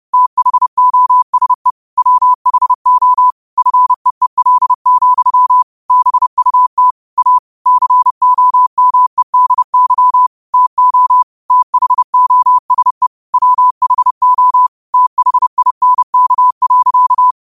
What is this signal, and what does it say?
Morse code, heard as a keyed tone, about 31 wpm, 1000 Hz. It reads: THOSE WHO FEEL, BUT A COMEDY TO THOSE WHO THINK.
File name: 30wpm